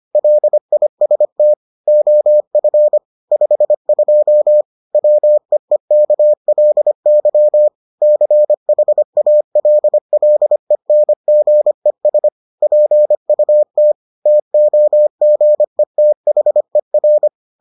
Morse code, heard as a keyed tone, about 25 wpm, 600 Hz. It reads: LIST OF 52 WEEKLY CHALLENGES PUT TOGETHER